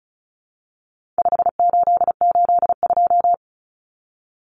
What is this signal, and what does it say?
Morse code, heard as a keyed tone, about 34 wpm, 700 Hz. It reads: 5882
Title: Numbers - 4-Digit 35wpm